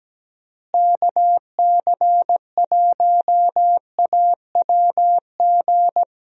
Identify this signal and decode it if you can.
Morse code, heard as a keyed tone, about 17 wpm, 700 Hz. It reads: KC1AWG
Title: USA Call Signs - 6 Characters - Repeated 3x - 17wpm